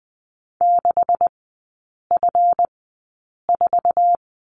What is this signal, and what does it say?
Morse code, heard as a keyed tone, about 20 wpm, spaced as for 8 wpm, 690 Hz. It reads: 6F4
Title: 198 Lesson 32e - Letter k - Three Characters 20wpm